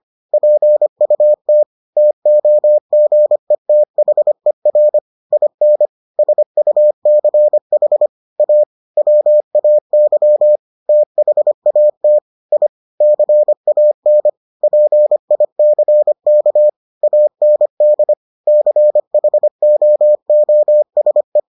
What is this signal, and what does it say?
Morse code, heard as a keyed tone, about 25 wpm, 600 Hz. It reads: PUT TOGETHER IN SUCH A WAY THAT I CAN PICK AND CHOOSE